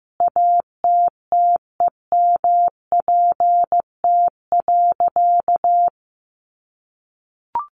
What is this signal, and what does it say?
Morse code, heard as a keyed tone, about 15 wpm, 700 Hz. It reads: ATTEMPT.
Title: Sentences from Top 1000 Words 15wpm